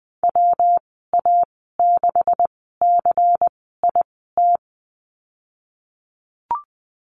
Morse code, heard as a keyed tone, about 20 wpm, spaced as for 15 wpm, 700 Hz. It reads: WA6CIT
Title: 186 Lesson 30g - Letter b - Call Signs 20wpm